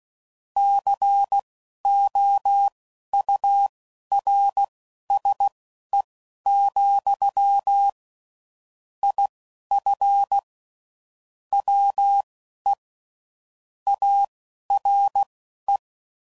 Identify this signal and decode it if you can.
Morse code, heard as a keyed tone, about 16 wpm, spaced as for 11 wpm, 800 Hz. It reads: COURSE, IF WE ARE
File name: CW Code Groups @10WPM / QSOs / QSO WA7QOX